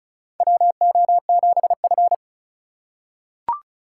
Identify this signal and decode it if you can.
Morse code, heard as a keyed tone, about 36 wpm, 700 Hz. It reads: WO7F
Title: USA Call Signs - 6x Character Spacing - 35wpm